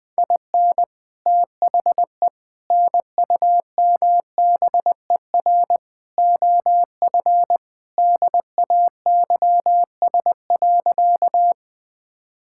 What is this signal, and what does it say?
Morse code, heard as a keyed tone, about 21 wpm, 700 Hz. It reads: IN THE NUMBER OF DAYS.
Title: Sentences with 4-Word Phrases - Spoken Beginning - 20wpm